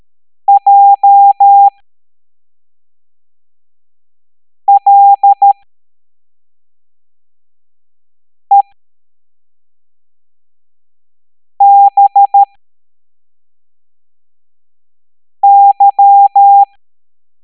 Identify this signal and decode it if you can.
Morse code, heard as a keyed tone, about 13 wpm, spaced as for 3 wpm, 800 Hz. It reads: JLEBY